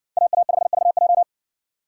Morse code, heard as a keyed tone, about 57 wpm, 700 Hz. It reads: AA5VJ